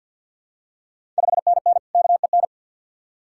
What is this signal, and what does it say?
Morse code, heard as a keyed tone, about 51 wpm, 690 Hz. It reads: 5NN KEN
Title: Ontario QSO Party Exchange 50wpm